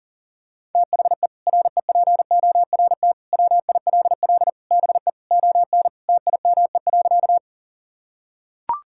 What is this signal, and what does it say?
Morse code, heard as a keyed tone, about 40 wpm, 700 Hz. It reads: THE REPORT WILL BE ON TIME.
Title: Sentences from Top 1000 Words - Review Part 6 - 40wpm